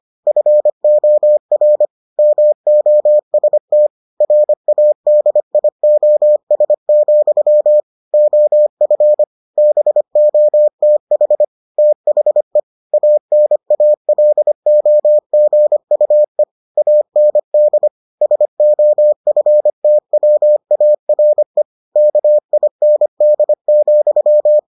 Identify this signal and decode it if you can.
Morse code, heard as a keyed tone, about 25 wpm, 600 Hz. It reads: FOR MOST RADIOS, OF BOTH THE ANALOGUE AND SOFTWARE KIND,